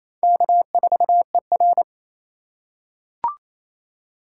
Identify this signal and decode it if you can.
Morse code, heard as a keyed tone, about 28 wpm, 700 Hz. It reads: K4ER
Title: USA Call Signs - 4 Characters - 6x Character Spacing - 28wpm